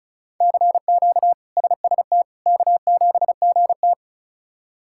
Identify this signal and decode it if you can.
Morse code, heard as a keyed tone, about 36 wpm, 700 Hz. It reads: CQ SST K7GT
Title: K1USN SST QSO Practice - 35wpm